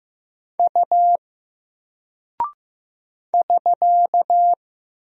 Morse code, heard as a keyed tone, about 15 wpm, 700 Hz. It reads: U <SK>
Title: Single Character (Letter:Number:Punctuation:Prosign) Alt 15wpm